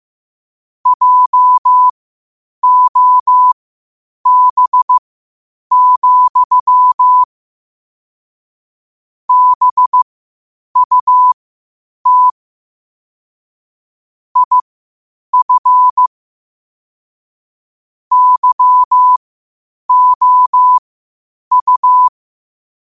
Morse code, heard as a keyed tone, about 15 wpm, spaced as for 8 wpm, 1000 Hz. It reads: JOB, BUT IF YOU
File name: qotdfiles / 8wpm